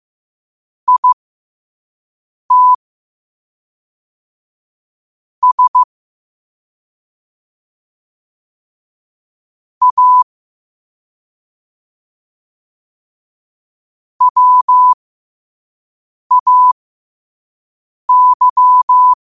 Morse code, heard as a keyed tone, about 15 wpm, spaced as for 5 wpm, 1000 Hz. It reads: IT S A WAY
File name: qotdfiles / 5wpm